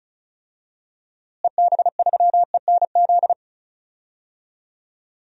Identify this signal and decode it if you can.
Morse code, heard as a keyed tone, about 34 wpm, 700 Hz. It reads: EB3ENZ
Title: DXCC-100 Call Signs - 6 Characters - Repeated 2x - 35wpm